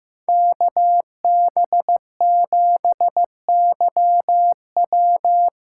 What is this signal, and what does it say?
Morse code, heard as a keyed tone, about 15 wpm, 700 Hz. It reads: KB7YW
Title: USA Call Signs - 6 Characters - 6x Character Spacing - 15wpm